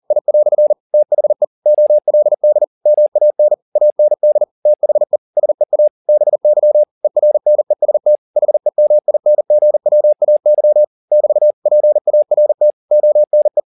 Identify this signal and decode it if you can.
Morse code, heard as a keyed tone, about 40 wpm, 600 Hz. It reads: I" THE OLD MAN AND THE SEA BY ERNEST HEMINGWAY <BT> PART ONE
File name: the old man and the sea 1 40wpm